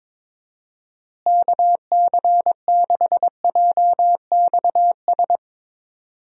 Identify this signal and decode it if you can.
Morse code, heard as a keyed tone, about 22 wpm, 700 Hz. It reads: KC6JXS
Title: USA Call Signs - 6 Characters - Repeated 2x - 22wpm